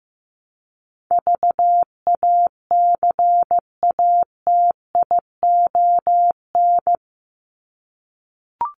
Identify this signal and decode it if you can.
Morse code, heard as a keyed tone, about 15 wpm, 700 Hz. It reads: VACATION
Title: Top 1501-1750 Words 15wpm